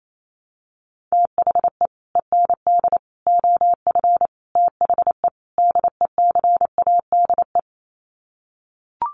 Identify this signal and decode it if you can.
Morse code, heard as a keyed tone, about 28 wpm, 700 Hz. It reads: THE END OF THE DECADE
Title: Sets of 5 Words Encore 28wpm